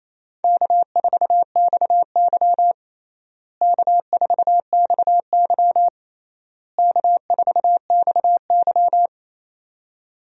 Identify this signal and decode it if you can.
Morse code, heard as a keyed tone, about 28 wpm, 700 Hz. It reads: K4XY K4XY K4XY
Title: USA Call Signs - 4 Characters - Repeated 3x - 28wpm